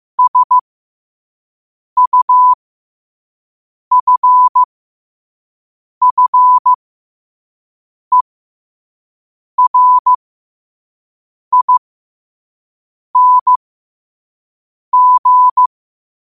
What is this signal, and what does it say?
Morse code, heard as a keyed tone, about 15 wpm, spaced as for 5 wpm, 1000 Hz. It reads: SUFFERING
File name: qotdfiles / 5wpm